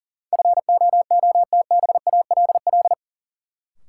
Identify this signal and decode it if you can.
Morse code, heard as a keyed tone, about 40 wpm, 700 Hz. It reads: FOOTBALL
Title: Sentences from Top 1600 Words - New Word Drill Alt - 40wpm